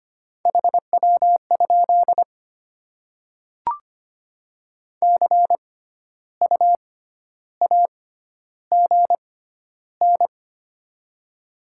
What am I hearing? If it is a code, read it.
Morse code, heard as a keyed tone, about 25 wpm, spaced as for 12 wpm, 700 Hz. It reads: HW? CU AGN
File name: Top 20 QSO Elements - 6x Character Spacing - 25wpm